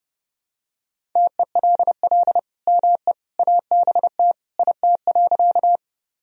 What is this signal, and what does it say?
Morse code, heard as a keyed tone, about 30 wpm, 700 Hz. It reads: TELL ME ABT IT.
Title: Sentences from Top 200 Words and Abbreviations 30wpm